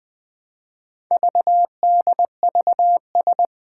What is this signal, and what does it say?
Morse code, heard as a keyed tone, about 20 wpm, 700 Hz. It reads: VDVS
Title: Four Challenging Letter-Number Combinations 20wpm